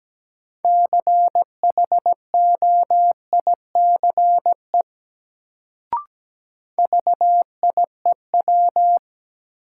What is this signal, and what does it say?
Morse code, heard as a keyed tone, about 17 wpm, 700 Hz. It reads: CHOICE VIEW
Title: Sets of 5 Words - New Word Drill - 17wpm